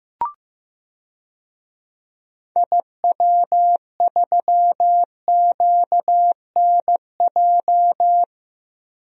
Morse code, heard as a keyed tone, about 16 wpm, 700 Hz. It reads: IW3QNJ
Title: DXCC-100 Call Signs - 6 Characters - Repeated 3x - 15wpm